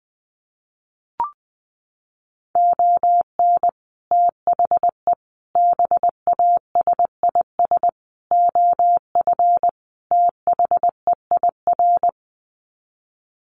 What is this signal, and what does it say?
Morse code, heard as a keyed tone, about 20 wpm, 700 Hz. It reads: ON THE BASIS OF THEIR
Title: Sets of 5 Words Encore 20wpm